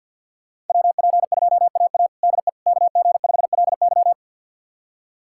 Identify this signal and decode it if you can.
Morse code, heard as a keyed tone, about 47 wpm, 700 Hz. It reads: WP2AA DE KG5LY
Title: USA Callsign DE Callsign - 5 Characters - 50wpm